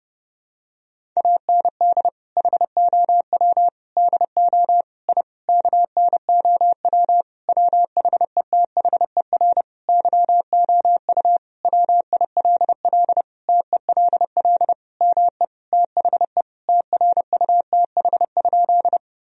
Morse code, heard as a keyed tone, about 30 wpm, 700 Hz. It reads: AND HOW DO I KNOW WHETHER YOU WILL TELL ME THE TRUTH?
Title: Sentences from Top 1000 Words - Review Part 3 - 30wpm